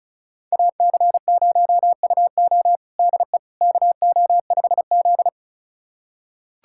Morse code, heard as a keyed tone, about 34 wpm, 700 Hz. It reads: AC0UO DE KO5Z
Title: USA Callsign DE Callsign 35wpm